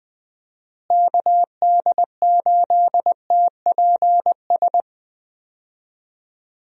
Morse code, heard as a keyed tone, about 20 wpm, 700 Hz. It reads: KD8TPS
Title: USA Call Signs - 6 Characters - Repeated 2x - 20wpm